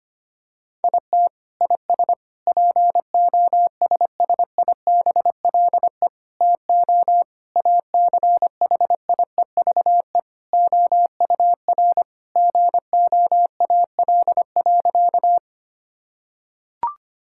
Morse code, heard as a keyed tone, about 25 wpm, 700 Hz. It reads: IT IS POSSIBLE TO ACHIEVE OUR GOAL.
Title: Sentences from Top 1000 Words - Review Part 6 - 25wpm